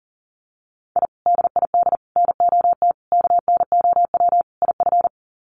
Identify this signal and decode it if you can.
Morse code, heard as a keyed tone, about 40 wpm, 700 Hz. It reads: I DID NOT KNOW IF
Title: Sets of 5 Words Encore 40wpm